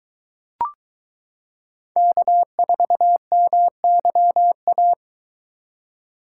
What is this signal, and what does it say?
Morse code, heard as a keyed tone, about 23 wpm, 700 Hz. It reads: K4MYA